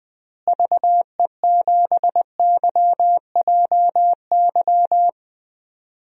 Morse code, heard as a keyed tone, about 21 wpm, 700 Hz. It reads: VE7YJY
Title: DXCC-100 Call Signs - 6 Characters - Repeated 2x - 20wpm